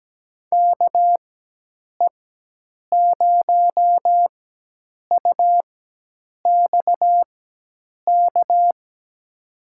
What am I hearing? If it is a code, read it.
Morse code, heard as a keyed tone, about 17 wpm, spaced as for 8 wpm, 700 Hz. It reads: KE0UXK
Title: USA Call Signs - 6 Characters - 4x Character Spacing - 17wpm